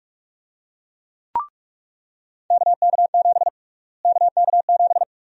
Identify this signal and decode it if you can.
Morse code, heard as a keyed tone, about 44 wpm, 700 Hz. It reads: KK7 KK7
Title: USA Call Sign Prefixes - Repeated 2x - 45wpm